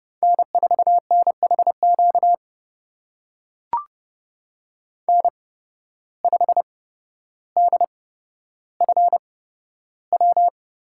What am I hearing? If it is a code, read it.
Morse code, heard as a keyed tone, about 31 wpm, spaced as for 13 wpm, 700 Hz. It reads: N4NHQ N5DFW
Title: USA Call Signs - 8x Character Spacing - 30wpm